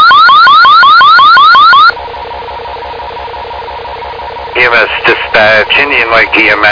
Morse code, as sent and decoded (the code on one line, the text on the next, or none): none